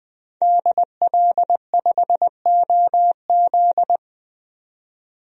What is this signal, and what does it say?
Morse code, heard as a keyed tone, about 20 wpm, 700 Hz. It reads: DL5OZ
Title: DXCC-100 Call Signs - 5 Characters - 20wpm